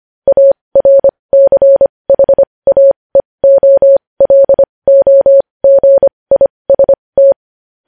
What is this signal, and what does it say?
Morse code, heard as a keyed tone, about 25 wpm, spaced as for 20 wpm, 550 Hz. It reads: ARCHAEOLOGIST
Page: news-headlines-in-morse-code-at-20-wpm